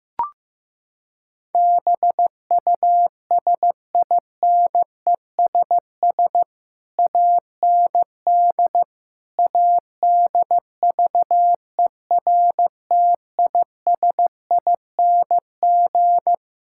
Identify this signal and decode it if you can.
Morse code, heard as a keyed tone, about 15 wpm, 700 Hz. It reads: BUSINESS AND ADVERTISING
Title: Sentences from Top 1000 Words - Review Part 6 - 15wpm